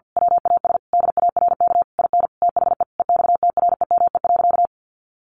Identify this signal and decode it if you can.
Morse code, heard as a keyed tone, about 50 wpm, 700 Hz. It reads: WAS DARK IN THE EXTREME.
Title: Sentences from Top 1300 Words 50wpm